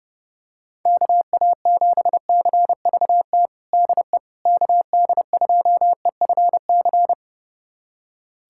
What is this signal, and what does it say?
Morse code, heard as a keyed tone, about 30 wpm, 700 Hz. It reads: KA7CVT DE KD2EFC
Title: USA Callsign DE Callsign 30wpm